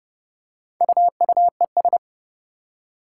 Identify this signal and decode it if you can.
Morse code, heard as a keyed tone, about 30 wpm, 700 Hz. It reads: UUES
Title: Four Challenging Letter-Number Combinations 30wpm